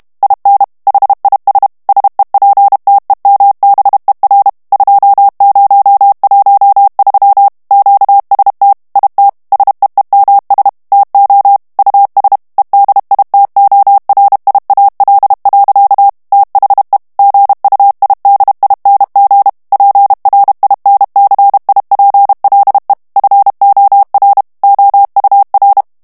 Morse code, heard as a keyed tone, about 32 wpm, 800 Hz. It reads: IN HIS SEPTEMBER 2013 QST IT SEEMS TO US EDITORIAL. THE GUIDING PRINCIPLE FOR OUR
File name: morse mp3 / ANGLAIS texte clair 3 vitesse 30 mots minute